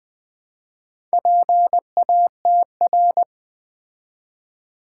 Morse code, heard as a keyed tone, about 20 wpm, 700 Hz. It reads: PATR
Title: Build A Word - 2 Letters At A Time - 6-Letter Words - 20wpm